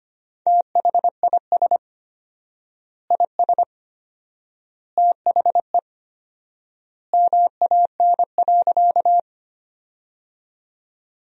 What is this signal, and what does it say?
Morse code, heard as a keyed tone, about 25 wpm, 700 Hz. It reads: THIS IS THE MAN.
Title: 4x Word-Spacing - Sentences from Top 200 Words - 25wpm